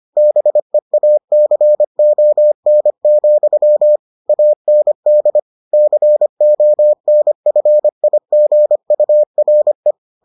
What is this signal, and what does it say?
Morse code, heard as a keyed tone, about 25 wpm, 600 Hz. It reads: BEACON, AND CONFIGURE